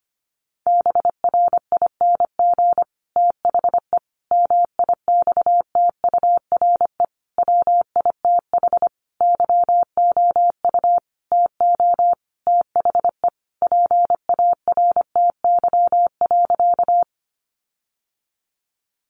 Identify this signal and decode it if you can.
Morse code, heard as a keyed tone, about 25 wpm, 700 Hz. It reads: BRING THE MIXTURE WITH YOU TO THE PARTY.
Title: Sentences from Top 1500 Words 25wpm